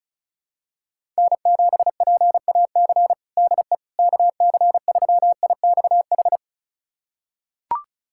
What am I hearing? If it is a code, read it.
Morse code, heard as a keyed tone, about 36 wpm, 700 Hz. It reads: N7PAC DE KC3IXH
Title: USA Callsign DE Callsign 35wpm